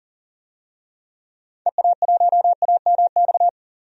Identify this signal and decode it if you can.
Morse code, heard as a keyed tone, about 40 wpm, 700 Hz. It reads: EA1AMX
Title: DXCC-100 Call Signs - 6 Characters - 40wpm